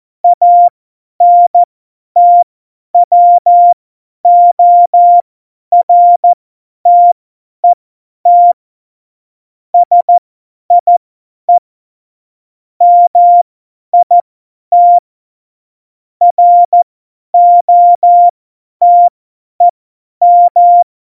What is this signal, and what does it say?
Morse code, heard as a keyed tone, about 14 wpm, spaced as for 10 wpm, 700 Hz. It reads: ANTWORTET SIE MIT ROTEM
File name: Morsekurs / Witze / Witz 29. September 2020, 50 F70